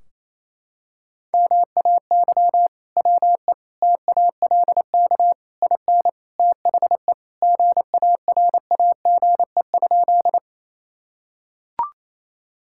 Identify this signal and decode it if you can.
Morse code, heard as a keyed tone, about 28 wpm, 700 Hz. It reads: MAY WE TALK IN THE GARAGE?
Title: Sentences from Top 1600 Words 28wpm